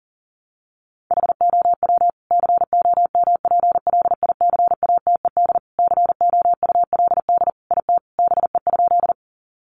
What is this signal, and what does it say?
Morse code, heard as a keyed tone, about 40 wpm, 700 Hz. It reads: HOW COMPLICATED COULD IT BE?
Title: Sentences from Top 1200 Words 40wpm